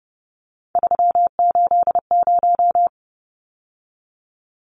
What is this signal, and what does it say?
Morse code, heard as a keyed tone, about 30 wpm, 700 Hz. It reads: 380
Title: Numbers - 3-Digit 30wpm